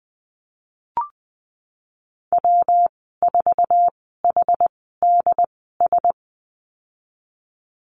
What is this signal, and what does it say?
Morse code, heard as a keyed tone, about 20 wpm, spaced as for 15 wpm, 700 Hz. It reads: W4HDS
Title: 147 Lesson 24g - Letter y - Call Signs 20wpm